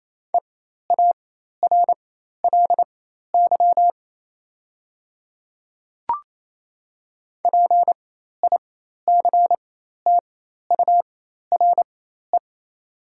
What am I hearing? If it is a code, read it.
Morse code, heard as a keyed tone, about 28 wpm, spaced as for 13 wpm, 690 Hz. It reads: EARLY PICTURE
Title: 4x Character Spacing - Top 500 Words - 28wpm